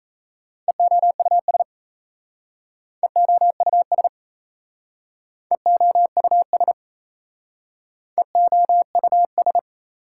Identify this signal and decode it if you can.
Morse code, heard as a keyed tone, about 37 wpm, 690 Hz. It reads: EOUS EOUS EOUS EOUS